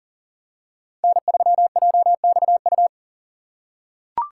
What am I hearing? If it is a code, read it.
Morse code, heard as a keyed tone, about 40 wpm, 700 Hz. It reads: N3JXU
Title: USA Call Signs - 5 Characters - 3x Character Spacing - 40wpm